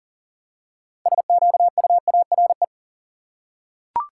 Morse code, heard as a keyed tone, about 40 wpm, 700 Hz. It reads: SQUARE
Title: Six Letter Words - 40wpm